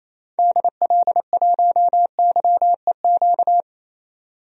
Morse code, heard as a keyed tone, about 28 wpm, 700 Hz. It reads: DL1YEQ
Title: DXCC-100 Call Signs - 6 Characters - 28wpm